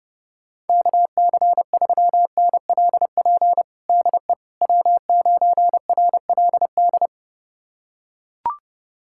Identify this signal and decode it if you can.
Morse code, heard as a keyed tone, about 30 wpm, 700 Hz. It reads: KC3NLP DE W9RLD
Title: USA Callsign DE Callsign 30wpm